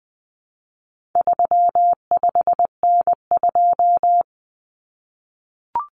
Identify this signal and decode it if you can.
Morse code, heard as a keyed tone, about 20 wpm, 700 Hz. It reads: 35N2
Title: Common Cut Numbers - 4-Digit 20wpm